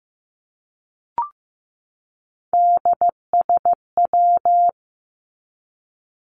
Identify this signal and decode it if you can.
Morse code, heard as a keyed tone, about 15 wpm, 700 Hz. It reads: DSW
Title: Abbreviations 15wpm